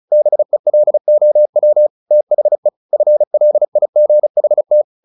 Morse code, heard as a keyed tone, about 35 wpm, 600 Hz. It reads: BELOW THE FLIGHT
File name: the old man and the sea / the old man and the sea 1 35wpm